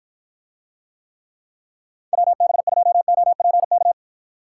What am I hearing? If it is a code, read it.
Morse code, heard as a keyed tone, about 53 wpm, 700 Hz. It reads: WB2OPK